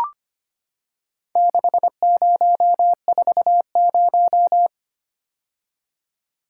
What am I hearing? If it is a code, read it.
Morse code, heard as a keyed tone, about 25 wpm, 690 Hz. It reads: 6040
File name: practice / 25wpm / Four Letter-Number Combinations - 25wpm